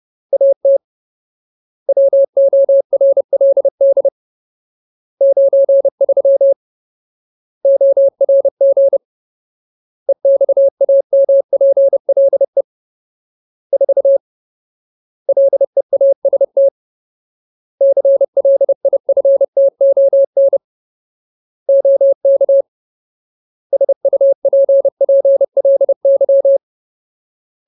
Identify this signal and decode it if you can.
Morse code, heard as a keyed tone, about 30 wpm, 550 Hz. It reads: AT WORLD 93 ORG EXAMPLE 4 LEAST CLIFTON OK SUPPLY
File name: Ultimate Word List / 30WPM Ultimate